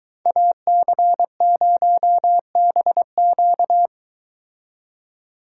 Morse code, heard as a keyed tone, about 22 wpm, 700 Hz. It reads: AC0BQ